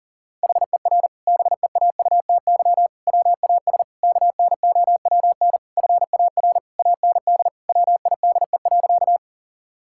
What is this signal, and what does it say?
Morse code, heard as a keyed tone, about 40 wpm, 690 Hz. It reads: HER BEAUTY WAS KNOWN FAR AND WIDE.
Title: Sentences from Top 1000 Words - Review Part 6 - 40wpm